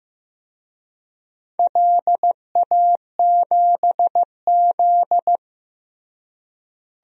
Morse code, heard as a keyed tone, about 15 wpm, 700 Hz. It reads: LA7Z
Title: DXCC-100 Call Signs - 4 Characters - Repeated 3x - 15wpm